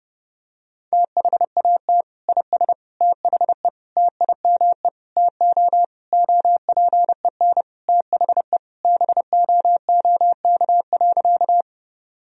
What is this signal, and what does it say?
Morse code, heard as a keyed tone, about 32 wpm, 700 Hz. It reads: THAT IS THE TIME TO OPEN THE BOOK.
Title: Sentences from Top 1000 Words - Review Part 1 - 30wpm